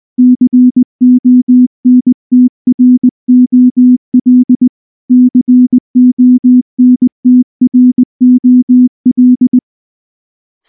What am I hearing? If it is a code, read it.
Morse code, heard as a keyed tone, about 19 wpm, 250 Hz. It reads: CONTROL CONTROL